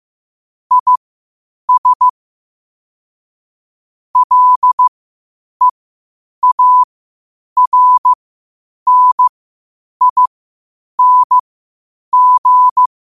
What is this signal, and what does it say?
Morse code, heard as a keyed tone, about 15 wpm, spaced as for 8 wpm, 1000 Hz. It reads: IS LEARNING
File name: qotdfiles / 8wpm